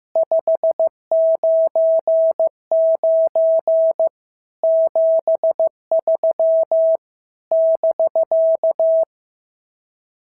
Morse code, heard as a keyed tone, about 14 wpm, 650 Hz. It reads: 599 73 <BK>
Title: SOTA QSO Practice 15wpm